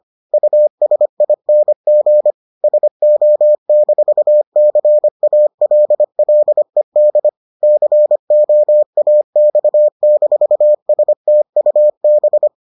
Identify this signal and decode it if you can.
Morse code, heard as a keyed tone, about 25 wpm, 600 Hz. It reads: USING SO-CALLED COAX-STUB